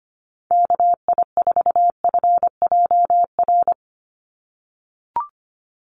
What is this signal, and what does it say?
Morse code, heard as a keyed tone, about 25 wpm, 700 Hz. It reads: KI4FJR
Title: USA Call Signs 25wpm